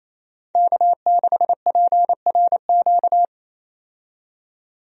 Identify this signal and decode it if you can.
Morse code, heard as a keyed tone, about 27 wpm, 700 Hz. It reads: K6PRQ
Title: USA Call Signs - 5 Characters - 28wpm